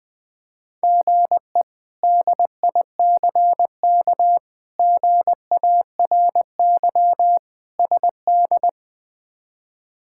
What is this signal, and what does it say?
Morse code, heard as a keyed tone, about 21 wpm, 700 Hz. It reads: GE DICK GARY SD
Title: K1USN SST QSO Practice - 20wpm